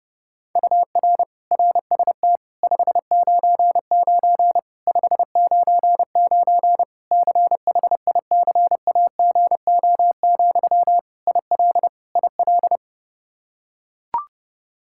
Morse code, heard as a keyed tone, about 32 wpm, 700 Hz. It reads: UR RST 599 599 CHICAGO, IL IL
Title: Practice QSO - Exchange 1 - Beginning - 30wpm